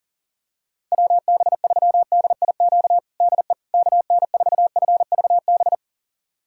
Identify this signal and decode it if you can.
Morse code, heard as a keyed tone, about 37 wpm, 700 Hz. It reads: WB3DIQ DE KN4FVB
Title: USA Callsign DE Callsign - 6 Characters - 40wpm